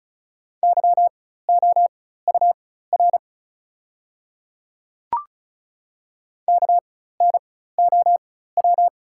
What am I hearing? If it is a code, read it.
Morse code, heard as a keyed tone, about 36 wpm, spaced as for 16 wpm, 700 Hz. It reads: YOUR KNOW
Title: 4x Character Spacing - Top 200 Words - 35wpm